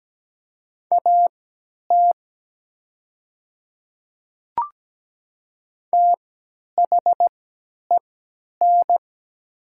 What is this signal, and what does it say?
Morse code, heard as a keyed tone, about 17 wpm, spaced as for 10 wpm, 700 Hz. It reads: AT THEN
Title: 3x Character Spacing - Top 100 Words - 17wpm